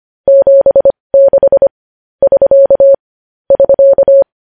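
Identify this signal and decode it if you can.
Morse code, heard as a keyed tone, about 25 wpm, spaced as for 20 wpm, 550 Hz. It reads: 76 <SK> <SK>
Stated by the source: News headlines in Morse at 20 wpm.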